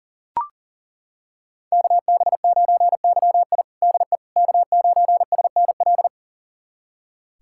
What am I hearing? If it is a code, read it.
Morse code, heard as a keyed tone, about 40 wpm, 700 Hz. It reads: KB9YI DE K9SNL
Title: USA Callsign DE Callsign - 5 Characters - 40wpm